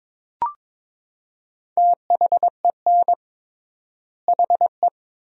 Morse code, heard as a keyed tone, about 22 wpm, 700 Hz. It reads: THEN HE
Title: 3x Word-Spacing - Sentences from Top 500 Words - 22wpm